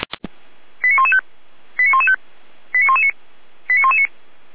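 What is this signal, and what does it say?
Five-tone selective call (ZVEI), heard as quick runs of five tones, tones of about 70 ms. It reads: 89115 89115 89119 89119
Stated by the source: ZVEI five-tone selective call 89084 received with an SDR.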